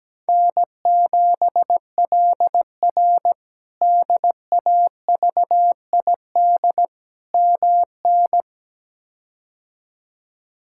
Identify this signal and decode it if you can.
Morse code, heard as a keyed tone, about 17 wpm, 700 Hz. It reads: N7LR DAVID MN
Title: K1USN SST QSO Practice - 17wpm